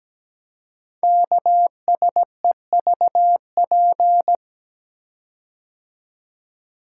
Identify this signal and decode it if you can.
Morse code, heard as a keyed tone, about 17 wpm, 700 Hz. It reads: KSEVP
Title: Code Groups - 5 Letter:Numbers - 17wpm